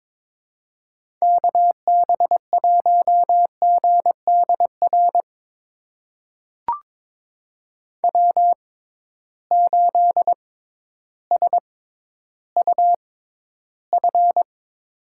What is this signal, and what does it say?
Morse code, heard as a keyed tone, about 22 wpm, spaced as for 12 wpm, 700 Hz. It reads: KB1GDR W8SUF
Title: USA Call Signs - 6x Character Spacing - 22wpm